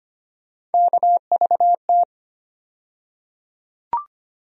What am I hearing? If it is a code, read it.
Morse code, heard as a keyed tone, about 25 wpm, 700 Hz. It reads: KVT
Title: USA Call Sign Postfixes - 3 Letters 25wpm